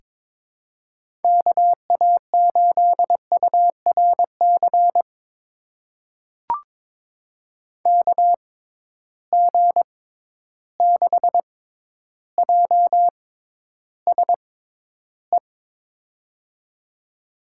Morse code, heard as a keyed tone, about 22 wpm, spaced as for 11 wpm, 700 Hz. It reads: KA8URC KG6JSE
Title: USA Call Signs - 6 Characters - 6x Character Spacing - 22wpm